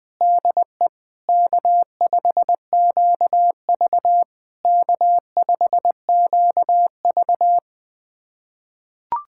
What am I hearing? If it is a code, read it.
Morse code, heard as a keyed tone, about 21 wpm, 660 Hz. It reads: DE K5QV K5QV
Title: SOTA QSO Practice 20wpm